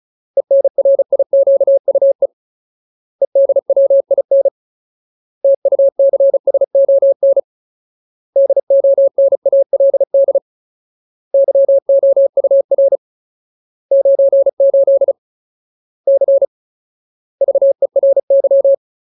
Morse code, heard as a keyed tone, about 34 wpm, 550 Hz. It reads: ENRIQUE EDWIN TUCSON DONALD YOUR 98 C VERY
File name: Ultimate Word List / 35WPM Ultimate